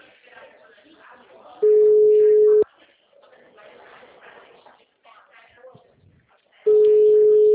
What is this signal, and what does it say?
Ringback tone (Europe, 425 Hz), with speech.